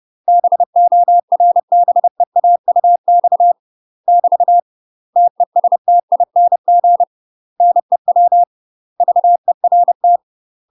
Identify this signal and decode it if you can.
Morse code, heard as a keyed tone, about 30 wpm, 700 Hz. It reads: BORBEAUX <BT> TESTING NEW VERT
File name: UK MASTER QSOs TRAINING / 30WPM / UK MASTER QSOs 0004